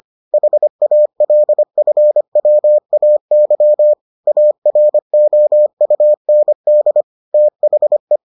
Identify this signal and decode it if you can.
Morse code, heard as a keyed tone, about 25 wpm, 600 Hz. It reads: HALFWAY AROUND THE